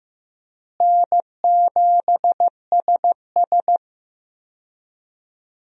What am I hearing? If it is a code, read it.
Morse code, heard as a keyed tone, about 15 wpm, 700 Hz. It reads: N7SS
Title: USA Call Signs - 4 Characters - Repeated 2x - 15wpm